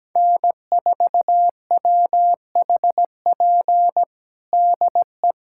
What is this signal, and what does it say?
Morse code, heard as a keyed tone, about 17 wpm, 700 Hz. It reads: N4WHP DE
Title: USA Callsign DE Callsign - 5 Characters - 17wpm